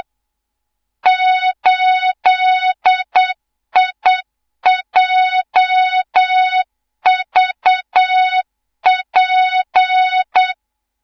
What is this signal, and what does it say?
Morse code, heard as a keyed tone, about 8 wpm, 3700 Hz, with noise. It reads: 8IJVP
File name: Morze-HA2MN / hang / 24 GYAKORLAT